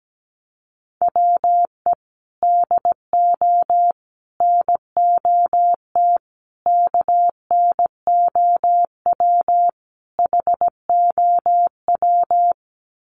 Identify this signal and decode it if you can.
Morse code, heard as a keyed tone, about 17 wpm, 700 Hz. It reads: WE DO NOT KNOW HOW
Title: Sets of 5 Words Encore 17wpm